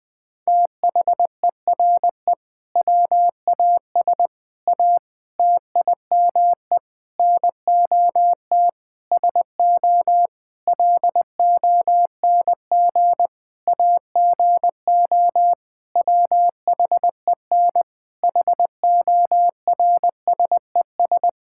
Morse code, heard as a keyed tone, about 20 wpm, 700 Hz. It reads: THERE WAS A TIME NOT SO LONG AGO WHEN HORSES
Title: Sentences with 8-Word Phrases - 20wpm